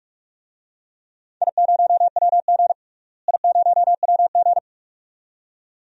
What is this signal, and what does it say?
Morse code, heard as a keyed tone, about 42 wpm, 700 Hz. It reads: I0WG I0WG
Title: DXCC-100 Call Signs - 4 Characters - Repeated 2x - 45wpm